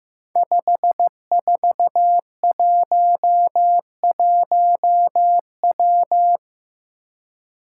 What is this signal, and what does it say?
Morse code, heard as a keyed tone, about 15 wpm, 700 Hz. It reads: 5411W
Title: Five Challenging Letter-Number Combinations 15wpm